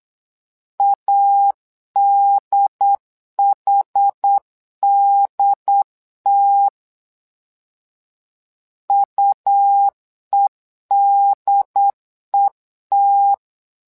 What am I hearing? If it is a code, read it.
Morse code, heard as a keyed tone, about 8 wpm, 800 Hz. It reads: ADHDT UEDET
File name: LEZIONI / 2 UHDN / 2 UHD 8wpm 2258